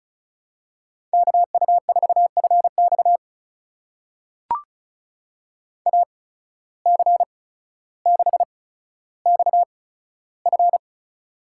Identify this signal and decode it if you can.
Morse code, heard as a keyed tone, about 36 wpm, spaced as for 15 wpm, 700 Hz. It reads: KU4FX AC6XF
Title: USA Call Signs - 5 Characters - 8x Character Spacing - 35wpm